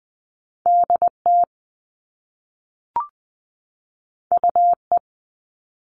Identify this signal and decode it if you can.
Morse code, heard as a keyed tone, about 20 wpm, 700 Hz. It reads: DT UE
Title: All Cut Numbers - 2-Digit 20wpm